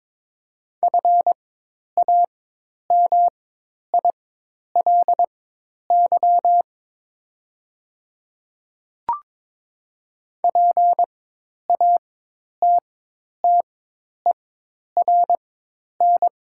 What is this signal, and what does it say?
Morse code, heard as a keyed tone, about 22 wpm, spaced as for 10 wpm, 700 Hz. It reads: FAMILY PATTERN
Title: 4x Character Spacing - Top 500 Words - 22wpm